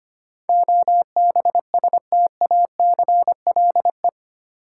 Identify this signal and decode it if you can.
Morse code, heard as a keyed tone, about 25 wpm, 700 Hz. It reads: OBSTACLE
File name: Eight Letter Words - 25wpm